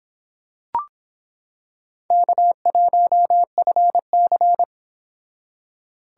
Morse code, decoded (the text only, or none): K1FC